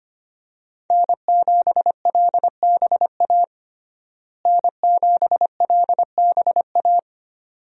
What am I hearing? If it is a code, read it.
Morse code, heard as a keyed tone, about 25 wpm, 700 Hz. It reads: N7LBA N7LBA
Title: USA Call Signs - 5 Characters - Repeated 2x - 25wpm